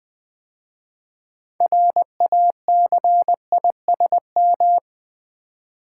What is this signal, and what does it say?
Morse code, heard as a keyed tone, about 20 wpm, 700 Hz. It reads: RACISM
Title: 6 Letter Words - First 3 Letters Given - 20wpm